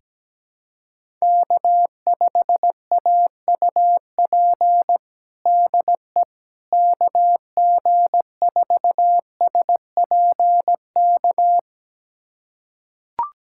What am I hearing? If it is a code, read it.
Morse code, heard as a keyed tone, about 17 wpm, 700 Hz. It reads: K5AUP DE KG4SPK
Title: USA Callsign DE Callsign 17wpm